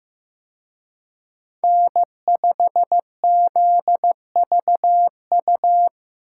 Morse code, heard as a keyed tone, about 18 wpm, 700 Hz. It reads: N5ZVU
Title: Speed Racing - USA 1x3 Call Signs 15wpm